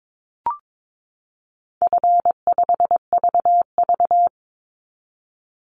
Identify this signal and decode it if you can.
Morse code, heard as a keyed tone, about 22 wpm, 700 Hz. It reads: F5VV
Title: Easy Call Signs Repeated 3x - 22wpm